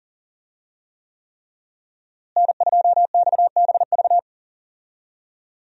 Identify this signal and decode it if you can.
Morse code, heard as a keyed tone, about 40 wpm, 700 Hz. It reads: N2XBV
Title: USA Call Signs - 5 Characters - Repeated 3x - 40wpm